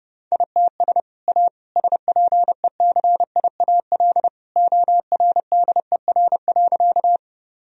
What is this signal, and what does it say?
Morse code, heard as a keyed tone, about 32 wpm, 700 Hz. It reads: ITS A SPECIAL ORDER.
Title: Sentences from Top 1000 Words - Review Part 3 - 30wpm